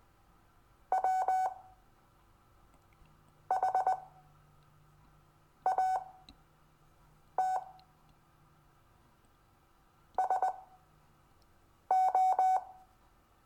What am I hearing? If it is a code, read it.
Morse code, heard as a keyed tone, about 20 wpm, spaced as for 5 wpm, 750 Hz. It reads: WHAT SO
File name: cw / Lesson22 REVIEW